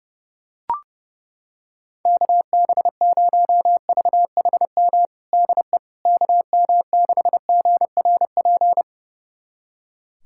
Morse code, heard as a keyed tone, about 30 wpm, 700 Hz. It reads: KB0VHM DE KM6GRP
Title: USA Callsign DE Callsign - 6 Characters - 30wpm